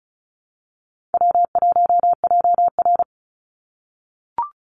Morse code, heard as a keyed tone, about 36 wpm, 700 Hz. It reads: W1JR
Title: Call Signs - 35wpm